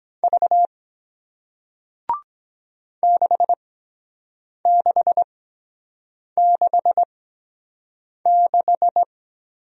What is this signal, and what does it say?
Morse code, heard as a keyed tone, about 23 wpm, spaced as for 7 wpm, 700 Hz. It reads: V 6666